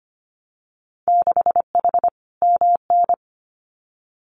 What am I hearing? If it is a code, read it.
Morse code, heard as a keyed tone, about 25 wpm, 700 Hz. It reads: 6H MN
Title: Winter Field Day Exchanges Encore 25wpm